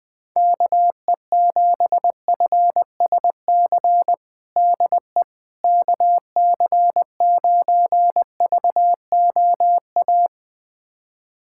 Morse code, heard as a keyed tone, about 20 wpm, 700 Hz. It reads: KE7FSC DE KC9VOA
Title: USA Callsign DE Callsign - 6 Characters - 20wpm